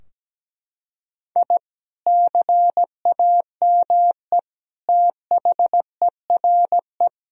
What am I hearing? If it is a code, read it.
Morse code, heard as a keyed tone, about 17 wpm, 700 Hz. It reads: I CAME THERE
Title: Sentences from Top 1000 Words - Review Part 4 - 17wpm